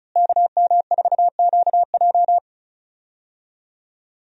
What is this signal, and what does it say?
Morse code, heard as a keyed tone, about 36 wpm, 700 Hz. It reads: KM4QJ
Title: USA Call Signs - 5 Characters - Repeated 2x - 35wpm